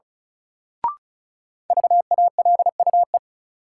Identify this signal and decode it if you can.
Morse code, heard as a keyed tone, about 36 wpm, 700 Hz. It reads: VALUE
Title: Sentences from Top 600 Words - New Word Drill - 35wpm